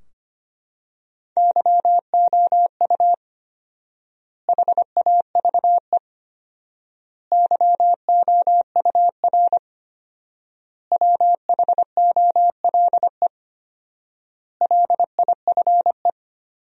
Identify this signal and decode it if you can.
Morse code, heard as a keyed tone, about 25 wpm, 700 Hz. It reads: YOU HAVE YOUR WHOLE LIFE
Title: 4x Word-Spacing - Sentences from Top 400 Words - 25wpm